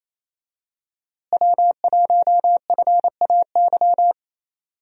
Morse code, heard as a keyed tone, about 27 wpm, 700 Hz. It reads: W1FAY
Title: USA Call Signs - 5 Characters - Repeated 2x - 28wpm